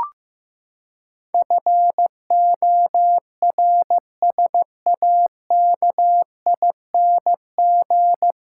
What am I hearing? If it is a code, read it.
Morse code, heard as a keyed tone, about 15 wpm, 700 Hz. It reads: FORSAKING
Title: Pangram Challenge v2 - 15wpm